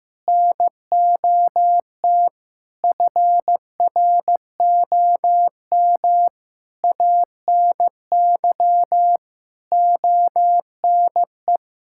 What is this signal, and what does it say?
Morse code, heard as a keyed tone, about 16 wpm, 700 Hz. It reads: NOT FROM ANY ONE
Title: Sentences from Top 1000 Words - Review Part 5 - 15wpm